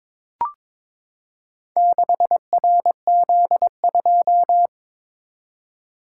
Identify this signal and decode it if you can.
Morse code, heard as a keyed tone, about 22 wpm, 700 Hz. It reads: 6RZ2
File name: practice / 22wpm / Four Letter-Number Combinations - 22wpm